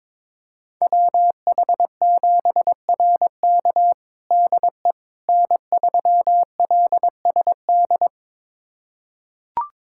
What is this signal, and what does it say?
Morse code, heard as a keyed tone, about 22 wpm, 700 Hz. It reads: WH7RK DE N3LSD
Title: USA Callsign DE Callsign - 5 Characters - 22wpm